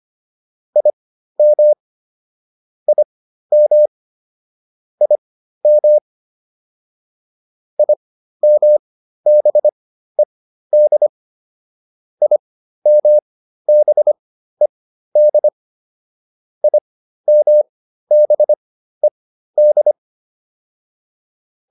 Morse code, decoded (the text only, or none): IM IM IM IMBED IMBED IMBED